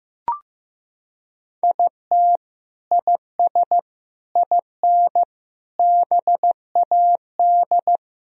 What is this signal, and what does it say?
Morse code, heard as a keyed tone, about 16 wpm, 700 Hz. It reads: IT IS IN BAD
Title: Sentences from Top 1000 Words - Review Part 1 - 15wpm